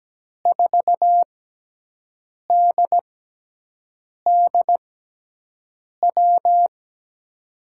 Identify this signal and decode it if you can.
Morse code, heard as a keyed tone, about 17 wpm, spaced as for 6 wpm, 700 Hz. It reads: 4DDW
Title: USA Call Signs - 6 Characters - 6x Character Spacing - 17wpm